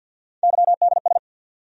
Morse code, heard as a keyed tone, about 47 wpm, 700 Hz. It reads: YDS
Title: USA Call Sign Postfixes - 3 Letters 50wpm